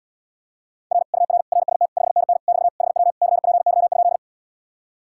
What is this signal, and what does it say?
Morse code, heard as a keyed tone, about 80 wpm, 700 Hz. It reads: R UR RST 5NN <BT> HW CPY?